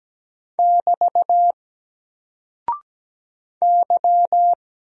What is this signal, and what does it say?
Morse code, heard as a keyed tone, about 17 wpm, 700 Hz. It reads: <BT> Y
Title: Single Character (Letter:Number:Punctuation:Prosign) 17wpm